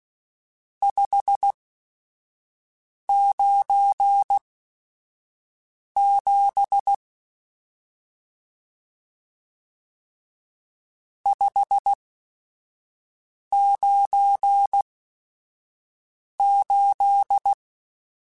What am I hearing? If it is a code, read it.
Morse code, heard as a keyed tone, about 16 wpm, spaced as for 5 wpm, 800 Hz. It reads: 597 598
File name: RSTs / RST-mp3s / Various RST @5